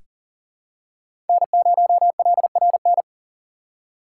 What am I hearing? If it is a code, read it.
Morse code, heard as a keyed tone, about 40 wpm, 700 Hz. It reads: N0LRN
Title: USA Call Signs - 6x Character Spacing - 40wpm